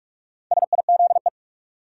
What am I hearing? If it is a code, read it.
Morse code, heard as a keyed tone, about 46 wpm, 700 Hz. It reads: SIZE
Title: Sets of 3 Words - New Word Drill - 45wpm